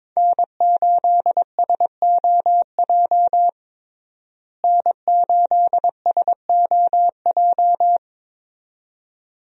Morse code, heard as a keyed tone, about 21 wpm, 700 Hz. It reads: N8SOJ N8SOJ
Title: USA Call Signs - 5 Characters - Repeated 2x - 22wpm